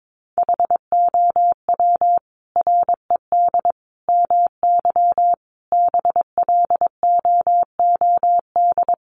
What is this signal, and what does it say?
Morse code, heard as a keyed tone, about 22 wpm, 700 Hz. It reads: HOW RED MY BLOOD
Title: Sentences from Top 1100 Words 22wpm